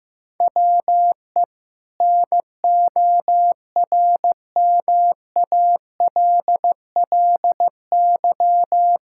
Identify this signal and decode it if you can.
Morse code, heard as a keyed tone, about 14 wpm, 700 Hz. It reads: WE NORMALLY
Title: Sentences from Top 1000 Words - Review Part 6 - 15wpm